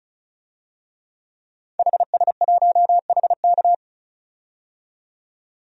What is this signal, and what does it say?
Morse code, heard as a keyed tone, about 36 wpm, 700 Hz. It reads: HS1HK
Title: DXCC-100 Call Signs - 5 Characters - 35wpm